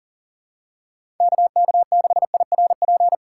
Morse code, heard as a keyed tone, about 40 wpm, 700 Hz. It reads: KK6IRP
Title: USA Call Signs - 6 Characters - 6x Character Spacing - 40wpm